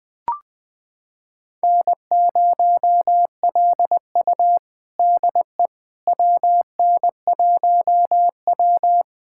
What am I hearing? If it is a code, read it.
Morse code, heard as a keyed tone, about 20 wpm, 700 Hz. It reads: N0LU DE WN1W
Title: USA Callsign DE Callsign - 4 Characters - 20wpm